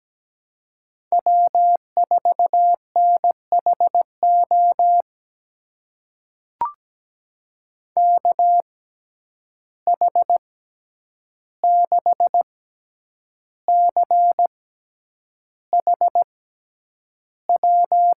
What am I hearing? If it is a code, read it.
Morse code, heard as a keyed tone, about 17 wpm, spaced as for 8 wpm, 700 Hz. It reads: W4NHO KH6CHW
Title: USA Call Signs - 6x Character Spacing - 17wpm